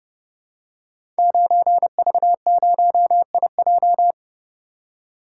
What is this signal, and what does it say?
Morse code, heard as a keyed tone, about 30 wpm, 700 Hz. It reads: 9V0IJ
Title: Code Groups - 5 Letter:Numbers - 30wpm